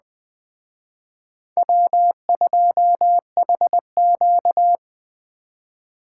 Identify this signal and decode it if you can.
Morse code, heard as a keyed tone, about 19 wpm, 700 Hz. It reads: W2HQ